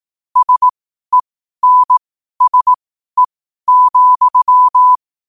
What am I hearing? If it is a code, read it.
Morse code, heard as a keyed tone, about 18 wpm, spaced as for 12 wpm, 1000 Hz. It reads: SENSE,
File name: qotdfiles / 12wpm